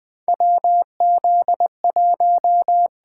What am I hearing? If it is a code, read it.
Morse code, heard as a keyed tone, about 20 wpm, 700 Hz. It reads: WZ1
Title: USA Call Signs - 4 Characters - 8x Character Spacing - 20wpm